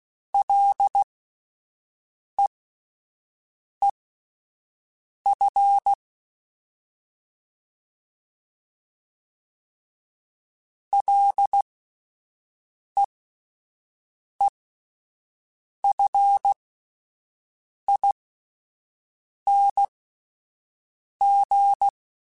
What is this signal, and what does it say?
Morse code, heard as a keyed tone, about 16 wpm, spaced as for 5 wpm, 800 Hz. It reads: LEEF LEEFING
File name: Alphabetical / Alphabetical mp3s / Ls @5